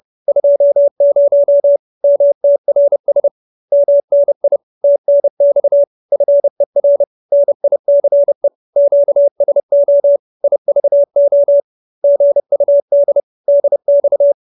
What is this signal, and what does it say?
Morse code, heard as a keyed tone, about 30 wpm, 570 Hz. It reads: 20 MTRS MNI TNX FER NICE QSO IVO GUD DX
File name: QSO319 30